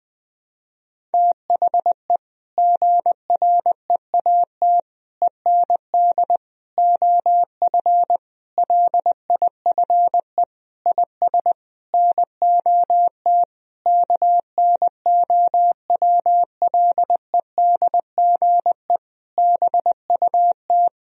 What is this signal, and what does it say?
Morse code, heard as a keyed tone, about 20 wpm, 700 Hz. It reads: THE GREAT END OF LIFE IS NOT KNOWLEDGE BUT
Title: Sentences from Top 1000 Words - Review Part 4 - 20wpm